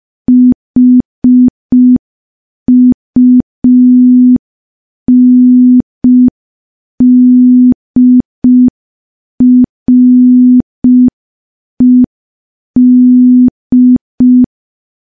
Morse code, heard as a keyed tone, about 5 wpm, 250 Hz. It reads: HUNDRED
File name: morse / morse-training-05WPM